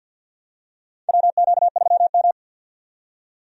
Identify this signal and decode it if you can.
Morse code, heard as a keyed tone, about 49 wpm, 700 Hz. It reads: WQ3M
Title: USA Call Signs - 4 Characters - 50wpm